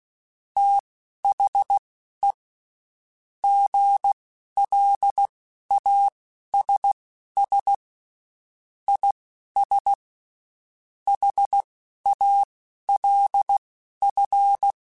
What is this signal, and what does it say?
Morse code, heard as a keyed tone, about 16 wpm, spaced as for 11 wpm, 800 Hz. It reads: THE GLASS IS HALF
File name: Spaced / Chuckles Spaced-mp3s / HalfFull sp @15